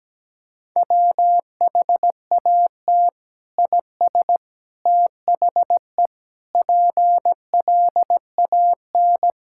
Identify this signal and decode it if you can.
Morse code, heard as a keyed tone, about 17 wpm, 700 Hz. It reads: WHAT IS THE PLAN
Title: Sentences from Top 1000 Words - Review Part 5 - 17wpm